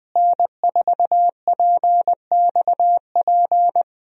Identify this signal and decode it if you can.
Morse code, heard as a keyed tone, about 20 wpm, 700 Hz. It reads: N4PXP
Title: USA Call Signs - 5 Characters - Repeated 3x - 20wpm